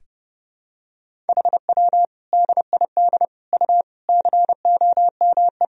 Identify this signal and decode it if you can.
Morse code, heard as a keyed tone, about 30 wpm, 700 Hz. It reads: HW DID U COME
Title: Sentences from Top 100 Words and Abbreviations 30wpm